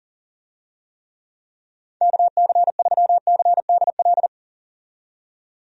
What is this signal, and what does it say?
Morse code, heard as a keyed tone, about 40 wpm, 700 Hz. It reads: KC3CDL
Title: USA Call Signs - 6 Characters - 40wpm